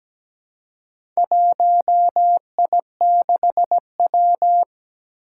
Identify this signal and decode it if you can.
Morse code, heard as a keyed tone, about 17 wpm, 700 Hz. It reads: 1I6W
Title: Four Challenging Letter-Number Combinations 17wpm